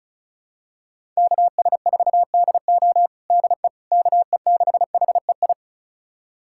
Morse code, heard as a keyed tone, about 34 wpm, 700 Hz. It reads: KS4DO DE KE6HEI
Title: USA Callsign DE Callsign 35wpm